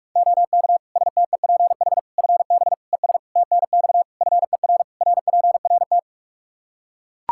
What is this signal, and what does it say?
Morse code, heard as a keyed tone, about 41 wpm, 700 Hz. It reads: OK STEPH FB ES TNX FER RPRT
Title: Practice QSO - Exchange 2 - Info - 45wpm